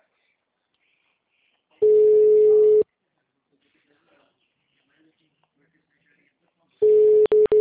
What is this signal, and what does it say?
Ringback tone (Europe, 425 Hz), with speech.
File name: out- -2014- - - .54608